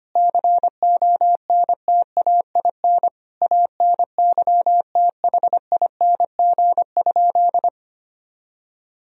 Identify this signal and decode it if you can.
Morse code, heard as a keyed tone, about 25 wpm, 700 Hz. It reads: CONTAIN ANYTHING?
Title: Sentences from Top 1000 Words - Review Part 2 - 25wpm